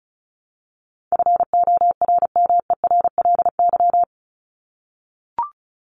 Morse code, heard as a keyed tone, about 34 wpm, 700 Hz. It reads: FORMERLY
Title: LY Suffix - Top 100 Words with Base Words 35wpm